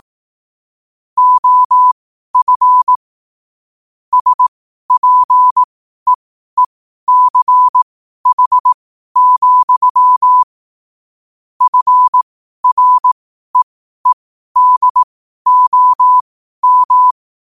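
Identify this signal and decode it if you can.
Morse code, heard as a keyed tone, about 18 wpm, spaced as for 12 wpm, 1000 Hz. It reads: OF SPEECH, FREEDOM
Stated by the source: 12 wpm Morse code.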